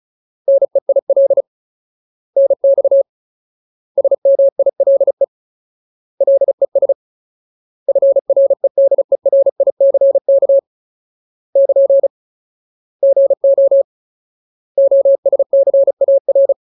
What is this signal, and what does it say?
Morse code, heard as a keyed tone, about 35 wpm, 550 Hz. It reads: NEIL NX SMILE LES FREDERICK <KN> GO OSCAR
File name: Ultimate Word List / 35WPM Ultimate